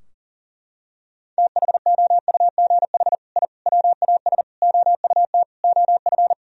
Morse code, heard as a keyed tone, about 40 wpm, 700 Hz. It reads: THOUGH I WAS OUT OF
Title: Sentences from Top 1000 Words - Review Part 4 - 40wpm